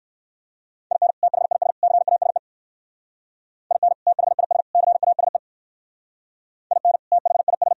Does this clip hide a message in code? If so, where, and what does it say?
Yes: Morse code from 0.9 to 7.7 s, heard as a keyed tone, about 62 wpm, 700 Hz. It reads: IN THIS CASE IN THIS CASE IN THIS